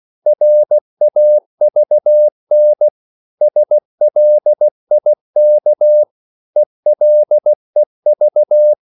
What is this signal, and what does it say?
Morse code, heard as a keyed tone, about 16 wpm, 600 Hz. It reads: RAVN SLIK ELEV
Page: Korte danske ord 16wpm